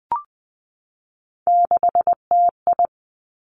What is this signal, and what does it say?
Morse code, heard as a keyed tone, about 20 wpm, 700 Hz. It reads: 6TI
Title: Three Challenging Letter-Number Combinations 20wpm